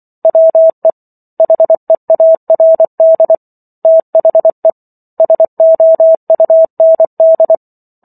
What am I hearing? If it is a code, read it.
Morse code, heard as a keyed tone, about 24 wpm, 650 Hz. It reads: WE HEARD THE SOUND